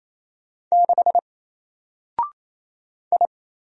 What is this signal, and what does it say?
Morse code, heard as a keyed tone, about 28 wpm, 700 Hz. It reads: 6 I
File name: Single-Letter-Number-28wpm